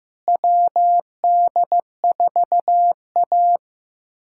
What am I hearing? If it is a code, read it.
Morse code, heard as a keyed tone, about 15 wpm, 700 Hz. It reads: WD4A
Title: USA Callsign DE Callsign - 4 Characters - 15wpm